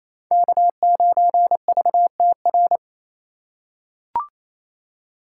Morse code, heard as a keyed tone, about 28 wpm, 700 Hz. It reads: K9VTR
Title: USA Call Signs - 5 Characters - 4x Character Spacing - 28wpm